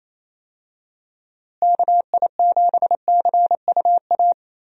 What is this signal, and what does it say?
Morse code, heard as a keyed tone, about 27 wpm, 700 Hz. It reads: KI7CUA
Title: USA Call Signs - 6 Characters - Repeated 2x - 28wpm